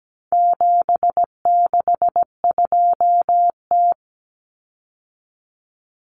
Morse code, heard as a keyed tone, about 17 wpm, 700 Hz. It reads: 762T
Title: Common Cut Numbers - 4-Digit 17wpm